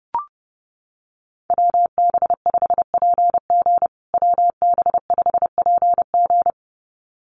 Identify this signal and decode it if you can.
Morse code, heard as a keyed tone, about 30 wpm, 700 Hz. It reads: WB5PG WB5PG
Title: USA Call Signs Repeated 2x 30wpm